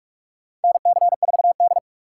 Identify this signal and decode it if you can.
Morse code, heard as a keyed tone, about 42 wpm, 700 Hz. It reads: NC4D
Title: USA Call Signs - 4 Characters - 4x Character Spacing - 45wpm